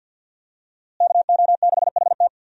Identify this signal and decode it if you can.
Morse code, heard as a keyed tone, about 50 wpm, 700 Hz. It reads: KO6HT
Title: USA Call Signs - 5 Characters - Repeated 2x - 50wpm